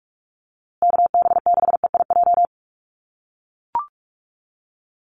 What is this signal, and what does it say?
Morse code, heard as a keyed tone, about 44 wpm, 700 Hz. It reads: KB6EIJ
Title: USA Call Signs Repeated 2x 45wpm